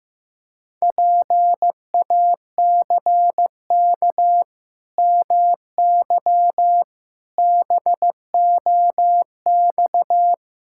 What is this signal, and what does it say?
Morse code, heard as a keyed tone, about 15 wpm, 700 Hz. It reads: PACK MY BOX
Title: Pangram Challenge v2 - 15wpm